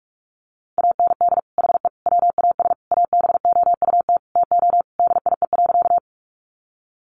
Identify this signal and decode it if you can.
Morse code, heard as a keyed tone, about 44 wpm, 700 Hz. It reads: AND HE WAS ABOUT TO DIE.
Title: Sentences from Top 1300 Words 45wpm